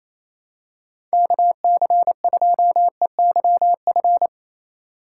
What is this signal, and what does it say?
Morse code, heard as a keyed tone, about 28 wpm, 700 Hz. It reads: KC2EYF
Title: USA Call Signs - 4x Character Spacing - 28wpm